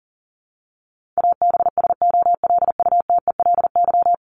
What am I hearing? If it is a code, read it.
Morse code, heard as a keyed tone, about 40 wpm, 700 Hz. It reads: ABSOLUTELY
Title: LY Suffix - Top 100 Words with Base Words 40wpm